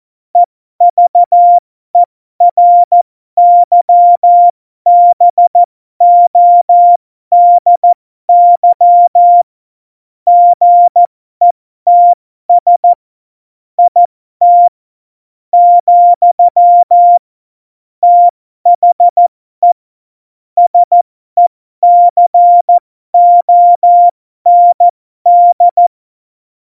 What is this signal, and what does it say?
Morse code, heard as a keyed tone, about 14 wpm, spaced as for 12 wpm, 700 Hz. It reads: EVERYBODY GETS IT , THE SECOND
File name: Morsekurs / Witze